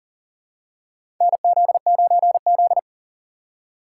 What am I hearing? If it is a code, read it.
Morse code, heard as a keyed tone, about 40 wpm, 700 Hz. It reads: NZ9Z
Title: USA Call Signs - 4 Characters - 3x Character Spacing - 40wpm